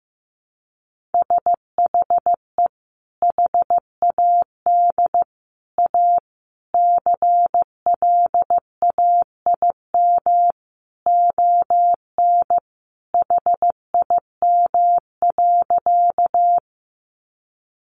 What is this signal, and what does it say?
Morse code, heard as a keyed tone, about 15 wpm, 700 Hz. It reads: SHE HAD A CLAIM ON HIM.
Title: Sentences from Top 1500 Words 15wpm